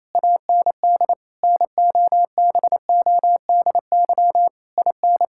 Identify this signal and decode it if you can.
Morse code, heard as a keyed tone, about 28 wpm, 700 Hz. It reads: AND NOBODY IN
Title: Sentences from Top 1600 Words 28wpm